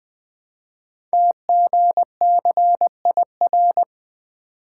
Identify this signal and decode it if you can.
Morse code, heard as a keyed tone, about 20 wpm, 700 Hz. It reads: TGCIR
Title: Code Groups - 5 Letters - 20wpm